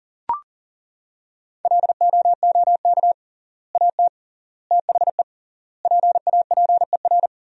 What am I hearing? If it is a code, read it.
Morse code, heard as a keyed tone, about 40 wpm, 700 Hz. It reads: LOOK AT THE PAPER
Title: 3x Word-Spacing - Sentences from Top 400 Words - 40wpm